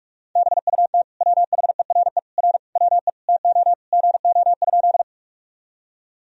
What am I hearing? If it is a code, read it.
Morse code, heard as a keyed tone, about 46 wpm, 700 Hz. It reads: BUT WHERE R WE TO GO?
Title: Sentences from Top 200 Words and Abbreviations 45wpm